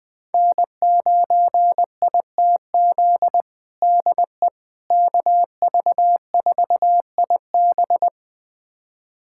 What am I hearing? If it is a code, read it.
Morse code, heard as a keyed tone, about 21 wpm, 700 Hz. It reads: N9ITZ DE KV4IB
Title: USA Callsign DE Callsign - 5 Characters - 20wpm